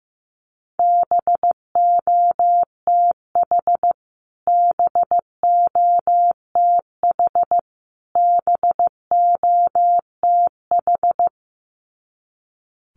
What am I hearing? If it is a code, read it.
Morse code, heard as a keyed tone, about 15 wpm, 700 Hz. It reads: BOTH BOTH BOTH
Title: Top 500 Words Repeated 15wpm